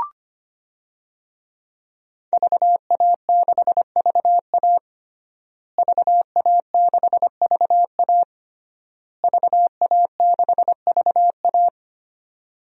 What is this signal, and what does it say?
Morse code, heard as a keyed tone, about 25 wpm, 700 Hz. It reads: VA6VA VA6VA VA6VA
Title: DXCC-100 Call Signs - 5 Characters - Repeated 3x - 25wpm